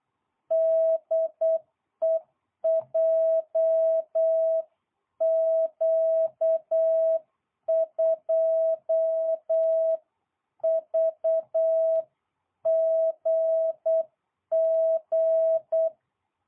Morse code, heard as a keyed tone, about 8 wpm, 660 Hz. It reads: DE JQ2VGG